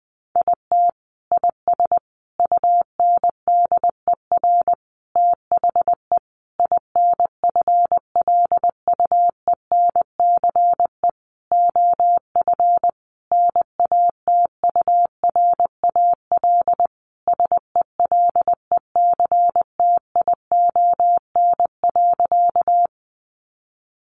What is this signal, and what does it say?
Morse code, heard as a keyed tone, about 20 wpm, 700 Hz. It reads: IT IS UNDER THE INFLUENCE OF NATURAL SELECTION.
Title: Sentences from Top 1400 Words 20wpm